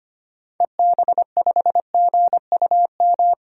Morse code, heard as a keyed tone, about 25 wpm, 700 Hz. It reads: EB5GUM
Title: DXCC-100 Call Signs - 6 Characters - 25wpm